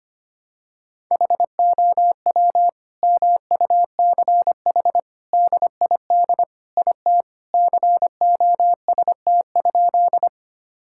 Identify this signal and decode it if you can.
Morse code, heard as a keyed tone, about 25 wpm, 700 Hz. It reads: HOW MUCH DID IT COST?
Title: Sentences from Top 1000 Words - Review Part 5 - 25wpm